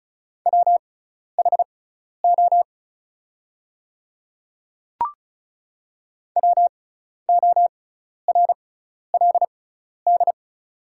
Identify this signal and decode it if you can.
Morse code, heard as a keyed tone, about 36 wpm, spaced as for 12 wpm, 700 Hz. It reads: WHO WORLD
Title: 6x Character Spacing - Top 200 Words - 35wpm